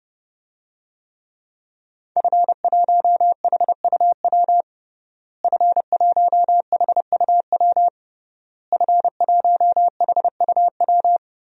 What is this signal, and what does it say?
Morse code, heard as a keyed tone, about 30 wpm, 700 Hz. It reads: F1HUW F1HUW F1HUW
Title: DXCC-100 Call Signs - 5 Characters - Repeated 3x - 30wpm